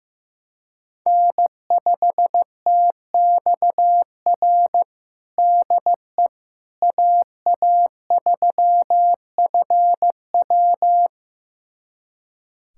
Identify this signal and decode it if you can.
Morse code, heard as a keyed tone, about 16 wpm, 700 Hz. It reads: N5TXR DE AA3FW
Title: USA Callsign DE Callsign - 5 Characters - 15wpm